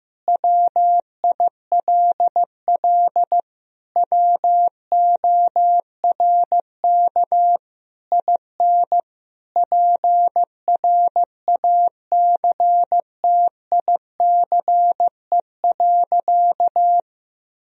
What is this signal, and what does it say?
Morse code, heard as a keyed tone, about 15 wpm, 700 Hz. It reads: WILL WORK IN PRACTICE.
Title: Sentences from Top 1000 Words - Review Part 4 - 15wpm